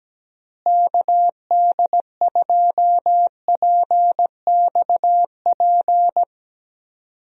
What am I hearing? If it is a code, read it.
Morse code, heard as a keyed tone, about 17 wpm, 700 Hz. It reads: KD2PXP
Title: USA Call Signs - 6 Characters - Repeated 3x - 17wpm